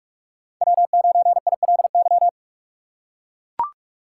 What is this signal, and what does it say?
Morse code, heard as a keyed tone, about 41 wpm, 700 Hz. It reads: W9ILY
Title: K1USN SST QSO Practice - 45wpm